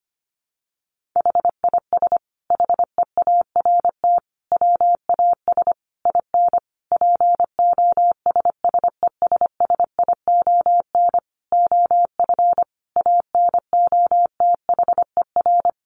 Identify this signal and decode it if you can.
Morse code, heard as a keyed tone, about 25 wpm, 700 Hz. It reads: HIS HEART WAS IN POSSESSION OF ANOTHER
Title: Sentences from Top 1400 Words 25wpm